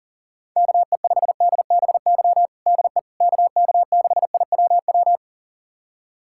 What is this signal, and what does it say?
Morse code, heard as a keyed tone, about 40 wpm, 700 Hz. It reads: KE5DBY DE KK6IWW
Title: USA Callsign DE Callsign - 6 Characters - 40wpm